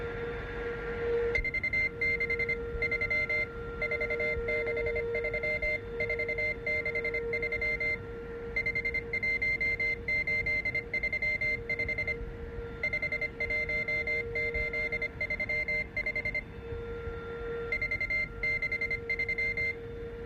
Morse code, heard as a keyed tone, about 25 wpm, 2100 Hz. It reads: 463 463 463 51835 51835 463